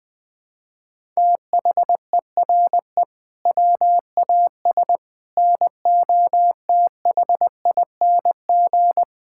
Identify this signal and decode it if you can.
Morse code, heard as a keyed tone, about 21 wpm, 700 Hz. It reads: THERE WAS NOTHING
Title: Sentences from Top 1000 Words - Review Part 6 - 20wpm